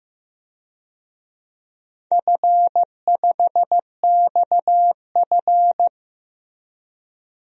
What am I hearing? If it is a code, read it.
Morse code, heard as a keyed tone, about 15 wpm, 700 Hz. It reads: F5XF
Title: DXCC-100 Call Signs - 4 Characters - Repeated 3x - 15wpm